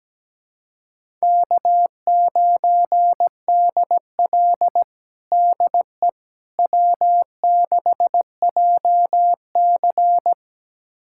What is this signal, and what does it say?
Morse code, heard as a keyed tone, about 17 wpm, 700 Hz. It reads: K9DL DE W6JC
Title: USA Callsign DE Callsign - 4 Characters - 17wpm